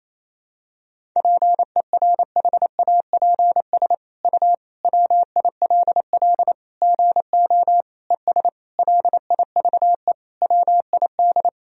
Morse code, heard as a keyed tone, about 28 wpm, 700 Hz. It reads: PERHAPS U WILL GO ES LIVE WID
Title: Sentences from Top 500 Words and Abbreviations 28wpm